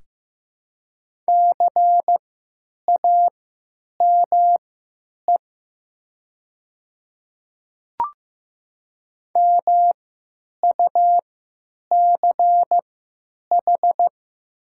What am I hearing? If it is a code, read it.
Morse code, heard as a keyed tone, about 15 wpm, spaced as for 9 wpm, 700 Hz. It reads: CAME MUCH
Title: 3x Character Spacing - Top 200 Words - 15wpm